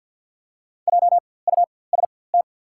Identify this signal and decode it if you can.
Morse code, heard as a keyed tone, about 48 wpm, spaced as for 23 wpm, 700 Hz. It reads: JUST
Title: 4x Character Spacing - Top 500 Words - 50wpm